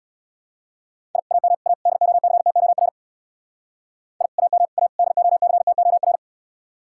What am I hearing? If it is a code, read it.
Morse code, heard as a keyed tone, about 66 wpm, 700 Hz. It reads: I AM A DOCTOR I AM A DOCTOR